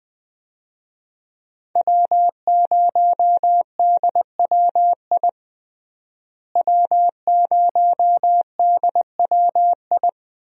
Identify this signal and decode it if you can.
Morse code, heard as a keyed tone, about 20 wpm, 700 Hz. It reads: W0DWI W0DWI
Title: USA Call Signs - 5 Characters - Repeated 2x - 20wpm